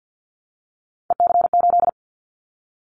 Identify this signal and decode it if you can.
Morse code, heard as a keyed tone, about 50 wpm, 700 Hz. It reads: EC8
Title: Callsign Prefixes for Top 100 DXCC - 50wpm